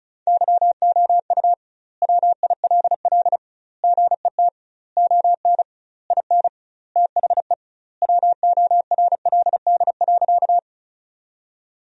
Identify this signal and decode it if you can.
Morse code, heard as a keyed tone, about 36 wpm, 700 Hz. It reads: YOU WILL GET ON IN THE WORLD.
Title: 2x Word-Spacing - Sentences from Top 200 Words - 35wpm